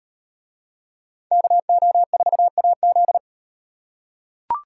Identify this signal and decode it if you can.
Morse code, heard as a keyed tone, about 36 wpm, 700 Hz. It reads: KO4AZ